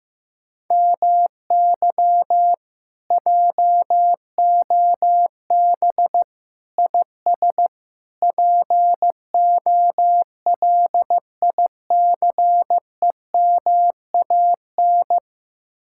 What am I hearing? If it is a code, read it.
Morse code, heard as a keyed tone, about 15 wpm, 700 Hz. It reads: MY JOB IS POLICEMAN
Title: Speed-Racing Common QSO Phrases Encore - 15wpm